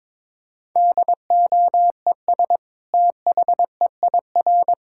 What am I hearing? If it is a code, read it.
Morse code, heard as a keyed tone, about 22 wpm, 700 Hz. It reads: DOES THEIR
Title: Sentences from Top 1000 Words - Review Part 5 - 22wpm